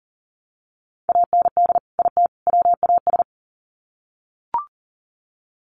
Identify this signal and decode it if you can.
Morse code, heard as a keyed tone, about 40 wpm, 700 Hz. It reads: AND IT WAS
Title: Sets of 3 Words 40wpm